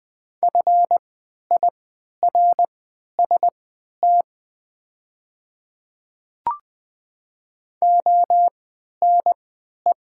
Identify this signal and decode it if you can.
Morse code, heard as a keyed tone, about 20 wpm, spaced as for 11 wpm, 700 Hz. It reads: FIRST ONE
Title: 3x Character Spacing - Top 100 Words - 20wpm